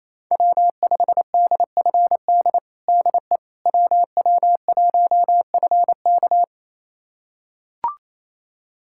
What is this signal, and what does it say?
Morse code, heard as a keyed tone, about 28 wpm, 700 Hz. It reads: W5DFD DE WW1FK
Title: USA Callsign DE Callsign - 5 Characters - 28wpm